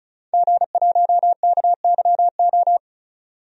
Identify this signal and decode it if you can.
Morse code, heard as a keyed tone, about 36 wpm, 700 Hz. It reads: G1KYO
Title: DXCC-100 Call Signs - 5 Characters - 35wpm